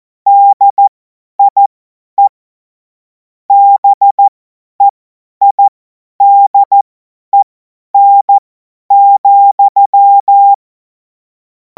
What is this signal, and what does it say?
Morse code, heard as a keyed tone, about 14 wpm, spaced as for 10 wpm, 800 Hz. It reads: DIE BEIDEN,
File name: Morsekurs / Witze / Witz 07. April 2026 50 F70